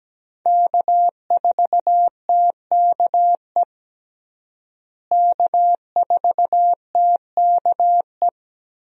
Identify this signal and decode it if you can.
Morse code, heard as a keyed tone, about 17 wpm, 700 Hz. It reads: K4TKE K4TKE
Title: USA Call Signs - 5 Characters - Repeated 3x - 17wpm